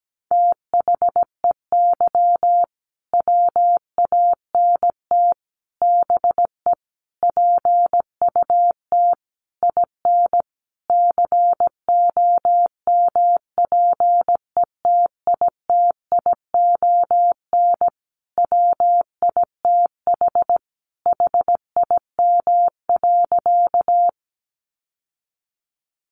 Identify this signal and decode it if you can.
Morse code, heard as a keyed tone, about 17 wpm, 700 Hz. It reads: THEY WANT BE PUT IN COMPETITION WITH HIM.
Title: Sentences from Top 1000 Words 17wpm